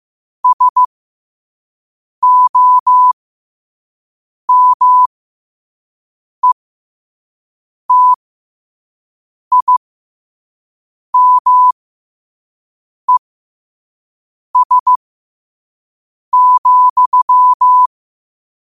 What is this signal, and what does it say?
Morse code, heard as a keyed tone, about 15 wpm, spaced as for 5 wpm, 1000 Hz. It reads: SOMETIMES,
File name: qotdfiles / 5wpm